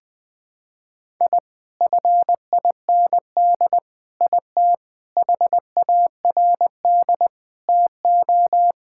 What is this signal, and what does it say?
Morse code, heard as a keyed tone, about 21 wpm, 700 Hz. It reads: I FIND IT HARD TO
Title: Sentences with 7-Word Phrases - 20wpm